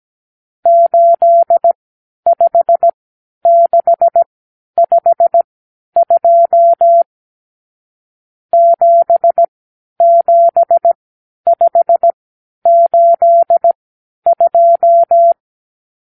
Morse code, heard as a keyed tone, about 17 wpm, spaced as for 10 wpm, 680 Hz. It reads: 85652 77582
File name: k7qo / Lesson 54 (9)